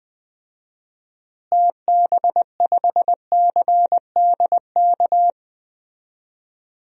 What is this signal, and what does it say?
Morse code, heard as a keyed tone, about 20 wpm, 700 Hz. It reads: TB5CDK
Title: DXCC-100 Call Signs - 6 Characters - Repeated 2x - 20wpm